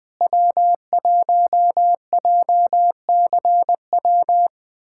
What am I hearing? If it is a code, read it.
Morse code, heard as a keyed tone, about 20 wpm, 700 Hz. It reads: W1JCW
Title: K1USN SST QSO Practice - 20wpm